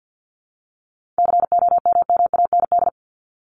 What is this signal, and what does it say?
Morse code, heard as a keyed tone, about 50 wpm, 700 Hz. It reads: COMMAND
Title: Top 1501-1750 Words 50wpm